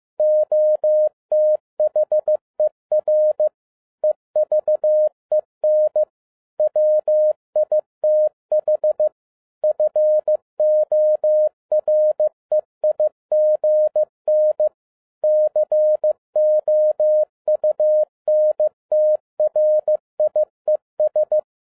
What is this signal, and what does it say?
Morse code, heard as a keyed tone, about 15 wpm, 610 Hz. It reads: OTHER EVEN WITH FOREIGN COUNTRIES